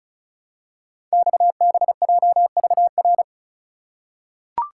Morse code, heard as a keyed tone, about 36 wpm, 700 Hz. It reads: XBJVR
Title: Code Groups - 5 Letters - 35wpm